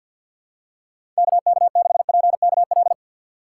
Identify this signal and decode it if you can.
Morse code, heard as a keyed tone, about 52 wpm, 700 Hz. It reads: KK6PKL
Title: USA Call Signs - 6 Characters - 8x Character Spacing - 50wpm